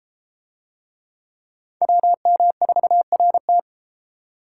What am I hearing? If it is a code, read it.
Morse code, heard as a keyed tone, about 31 wpm, 700 Hz. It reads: WM4RT